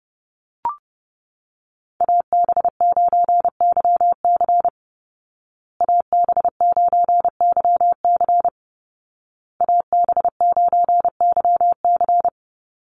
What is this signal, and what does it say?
Morse code, heard as a keyed tone, about 30 wpm, 700 Hz. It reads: AB9YC AB9YC AB9YC
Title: Call Signs Repeated 2x - 30wpm